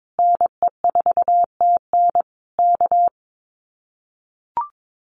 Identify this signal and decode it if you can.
Morse code, heard as a keyed tone, about 22 wpm, 700 Hz. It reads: NE4TN K
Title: Common QSO Phrases Encore 22wpm